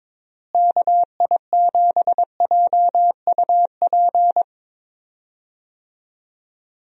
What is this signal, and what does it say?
Morse code, heard as a keyed tone, about 22 wpm, 700 Hz. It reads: KI7JUP
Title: USA Call Signs - 6 Characters - Repeated 2x - 22wpm